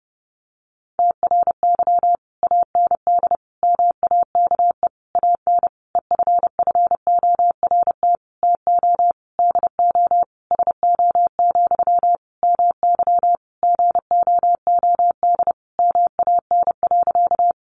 Morse code, heard as a keyed tone, about 30 wpm, 700 Hz. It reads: TRY AND MAKE AN EFFORT TO DO SO, MY GOOD MAN.
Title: Sentences from Top 1200 Words 30wpm